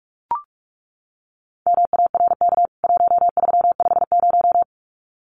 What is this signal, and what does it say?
Morse code, heard as a keyed tone, about 44 wpm, 700 Hz. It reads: MARK 1350
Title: CWT Exchange 45wpm